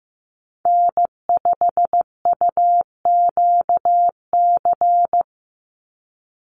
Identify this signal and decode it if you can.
Morse code, heard as a keyed tone, about 15 wpm, 700 Hz. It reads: N5UQC
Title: USA Call Signs 15wpm